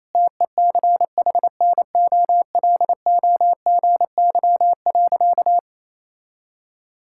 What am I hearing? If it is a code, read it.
Morse code, heard as a keyed tone, about 28 wpm, 700 Hz. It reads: TECHNOLOGY.
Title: Sentences from Top 1000 Words - Review Part 5 - 28wpm